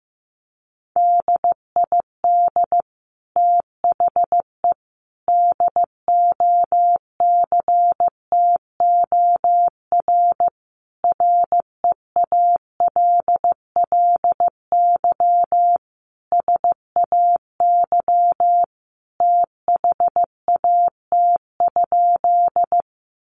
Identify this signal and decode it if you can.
Morse code, heard as a keyed tone, about 15 wpm, 700 Hz. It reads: DID THE DOCTOR REALLY SAY THAT?
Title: Sentences from Top 1500 Words 15wpm